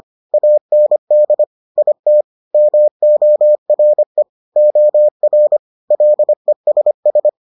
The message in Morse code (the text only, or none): AND IT MORE OR LESS